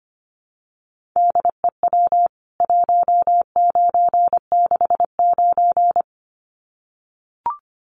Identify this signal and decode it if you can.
Morse code, heard as a keyed tone, about 25 wpm, 700 Hz. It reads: DEW 1969
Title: CWT Exchange 25wpm